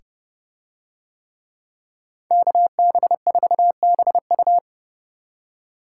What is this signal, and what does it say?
Morse code, heard as a keyed tone, about 24 wpm, 700 Hz. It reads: KB4BU
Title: Speed Racing - USA 2x2 Call Signs 20wpm